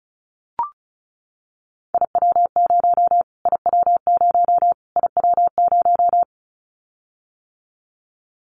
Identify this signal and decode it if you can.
Morse code, heard as a keyed tone, about 34 wpm, 700 Hz. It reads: IW0 IW0 IW0
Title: Callsign Prefixes for Top 25 DXCC - Repeated 3x - 35wpm